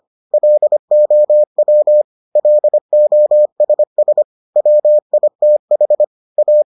Morse code, heard as a keyed tone, about 25 wpm, 600 Hz. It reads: LOW LOSS WITH A